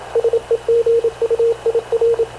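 Morse code, heard as a keyed tone, about 26 wpm, 460 Hz, with radio noise. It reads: SEGUIR
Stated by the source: Havana Radio CLA press (PX) broadcast, recorded by Radio Officer Paul Dumesnil aboard mv CNAV QUEST/CZDO.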